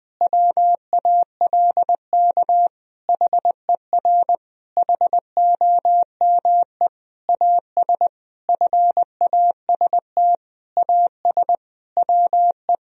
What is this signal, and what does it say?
Morse code, heard as a keyed tone, about 20 wpm, 700 Hz. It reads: WALK HER HOME AS FAST AS WE
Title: Sentences from Top 1000 Words - Review Part 2 - 20wpm